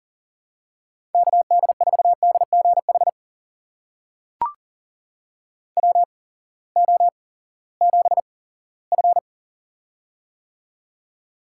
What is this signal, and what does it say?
Morse code, heard as a keyed tone, about 40 wpm, spaced as for 20 wpm, 700 Hz. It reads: KD4DGH WO7F
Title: USA Call Signs - 8x Character Spacing - 40wpm